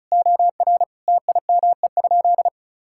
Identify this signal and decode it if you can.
Morse code, heard as a keyed tone, about 34 wpm, 700 Hz. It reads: OR TIME?
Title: Sentences from Top 1000 Words - Review Part 5 - 35wpm